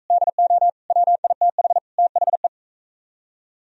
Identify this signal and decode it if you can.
Morse code, heard as a keyed tone, about 43 wpm, 690 Hz. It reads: DO WITH THE